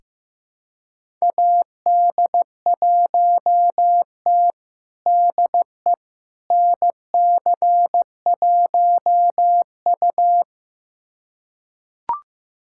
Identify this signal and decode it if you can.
Morse code, heard as a keyed tone, about 15 wpm, 700 Hz. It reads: AD1T DE NC1U
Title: USA Callsign DE Callsign - 4 Characters - 15wpm